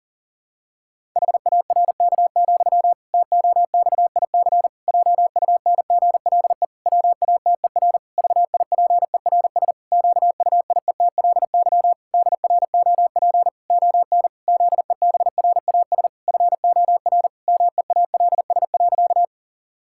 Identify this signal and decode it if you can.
Morse code, heard as a keyed tone, about 37 wpm, 700 Hz. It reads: HARK, TOXIC JUNGLE WATER VIPERS QUIETLY DROP ON ZEBRAS FOR MEALS.
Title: Pangram Challenge v2 - 40wpm